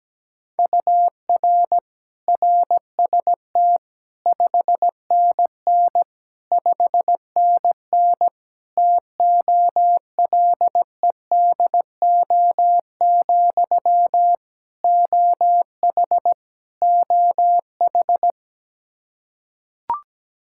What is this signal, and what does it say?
Morse code, heard as a keyed tone, about 17 wpm, 700 Hz. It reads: UR RST 5NN 5NN TOLEDO, OH OH
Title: Practice QSO - Exchange 1 - Beginning - 17wpm